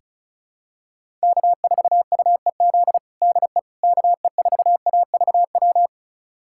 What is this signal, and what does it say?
Morse code, heard as a keyed tone, about 34 wpm, 690 Hz. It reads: K4UEZ DE KE4AVW
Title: USA Callsign DE Callsign 35wpm